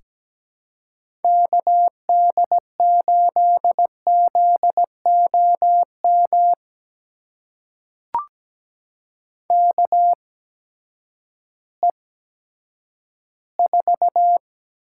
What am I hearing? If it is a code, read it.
Morse code, heard as a keyed tone, about 17 wpm, spaced as for 10 wpm, 700 Hz. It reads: KD8ZOM KE4
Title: USA Call Signs - 6 Characters - 8x Character Spacing - 17wpm